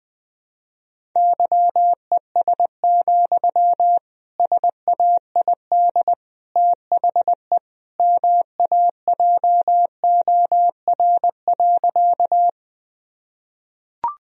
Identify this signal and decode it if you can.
Morse code, heard as a keyed tone, about 20 wpm, 700 Hz. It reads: YES, SAID THE MAJOR.
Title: Sentences from Top 1000 Words - Review Part 4 - 20wpm